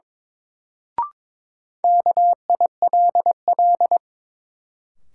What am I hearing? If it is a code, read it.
Morse code, heard as a keyed tone, about 21 wpm, 700 Hz. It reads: KILL
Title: Sentences from Top 1400 Words - New Word Drill Alt - 22wpm